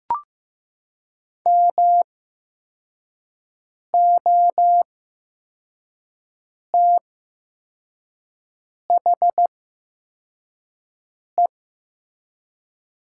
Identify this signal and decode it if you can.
Morse code, heard as a keyed tone, about 15 wpm, spaced as for 4 wpm, 700 Hz. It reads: MOTHE
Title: 8x Character Spacing - Top 300 Words - 15wpm